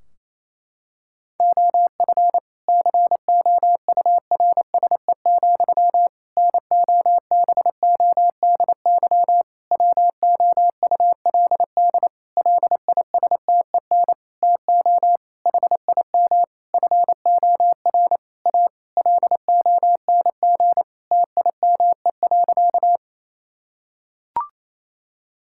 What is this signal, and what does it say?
Morse code, heard as a keyed tone, about 27 wpm, 700 Hz. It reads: OF COURSE, NOBODY WOULD LISTEN TO HIM FOR A LONG TIME.
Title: Sentences from Top 1600 Words 28wpm